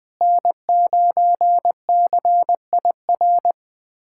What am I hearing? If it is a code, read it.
Morse code, heard as a keyed tone, about 20 wpm, 700 Hz. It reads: N9CIR
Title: USA Callsign DE Callsign - 5 Characters - 20wpm